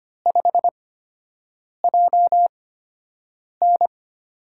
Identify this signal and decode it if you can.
Morse code, heard as a keyed tone, about 25 wpm, spaced as for 7 wpm, 700 Hz. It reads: 5JN
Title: USA Call Signs - 5 Characters - 8x Character Spacing - 25wpm